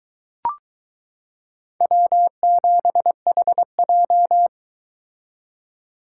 Morse code, heard as a keyed tone, about 23 wpm, 700 Hz. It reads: W7HJ